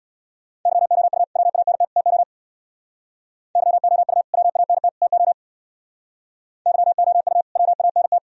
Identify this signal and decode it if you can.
Morse code, heard as a keyed tone, about 62 wpm, 700 Hz. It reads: YOU WANT TO YOU WANT TO YOU WANT
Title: Speed-Racing - Sets of 3 Words 50wpm